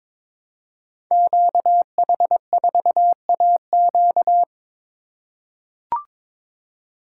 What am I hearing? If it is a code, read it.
Morse code, heard as a keyed tone, about 22 wpm, 700 Hz. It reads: QH4AQ
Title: Code Groups - 5 Letter:Numbers - 22wpm